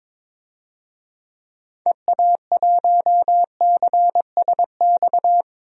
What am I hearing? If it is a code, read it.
Morse code, heard as a keyed tone, about 22 wpm, 700 Hz. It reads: EA1CSX
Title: DXCC-100 Call Signs - 6 Characters - 22wpm